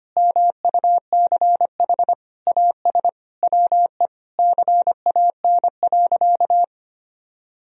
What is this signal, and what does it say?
Morse code, heard as a keyed tone, about 25 wpm, 700 Hz. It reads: MUCH AS WE CAN.
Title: Sentences from Top 1000 Words - Review Part 3 - 25wpm